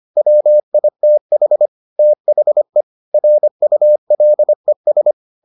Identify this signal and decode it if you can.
Morse code, heard as a keyed tone, about 25 wpm, 600 Hz. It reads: WITH THE RULES